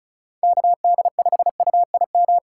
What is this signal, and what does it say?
Morse code, heard as a keyed tone, about 36 wpm, 700 Hz. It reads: KD5UIM
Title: USA Call Signs - 6 Characters - 3x Character Spacing - 35wpm